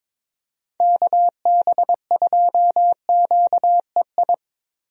Morse code, heard as a keyed tone, about 22 wpm, 700 Hz. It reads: KB2QEI
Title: USA Call Signs - 6 Characters - Repeated 2x - 22wpm